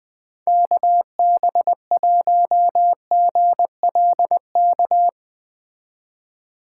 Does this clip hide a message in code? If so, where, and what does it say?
Yes: Morse code from 0.5 to 5.1 s, heard as a keyed tone, about 21 wpm, 700 Hz. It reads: KB1GLK